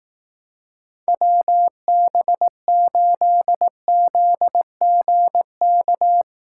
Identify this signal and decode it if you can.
Morse code, heard as a keyed tone, about 18 wpm, 700 Hz. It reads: WB8ZGK
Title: Speed Racing - USA 2x3 Call Signs 15wpm